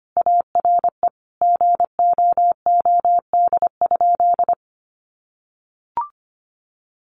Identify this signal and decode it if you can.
Morse code, heard as a keyed tone, about 25 wpm, 700 Hz. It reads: ARE GOOD?
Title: Sentences from Top 1300 Words 25wpm